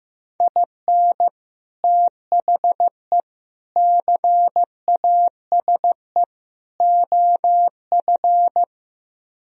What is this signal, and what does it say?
Morse code, heard as a keyed tone, about 15 wpm, 700 Hz. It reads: IN THE CASE OF
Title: Speed-Racing - Sets of 4 Words 15wpm